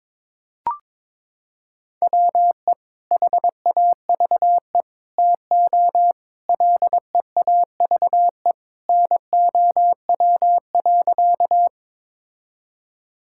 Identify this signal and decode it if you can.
Morse code, heard as a keyed tone, about 21 wpm, 700 Hz. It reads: WE HAVE TO LEAVE NOW.
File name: practice / 22wpm / Sentences with 3-Word Phrases - 22wpm